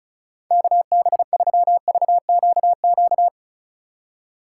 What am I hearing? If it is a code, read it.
Morse code, heard as a keyed tone, about 36 wpm, 700 Hz. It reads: KB3VQQ
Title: USA Call Signs - 6 Characters - 3x Character Spacing - 35wpm